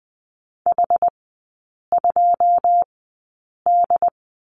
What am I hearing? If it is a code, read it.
Morse code, heard as a keyed tone, about 20 wpm, spaced as for 8 wpm, 700 Hz. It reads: H2D
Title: 081 Lesson 14e - Number 2 - Three Characters 20wpm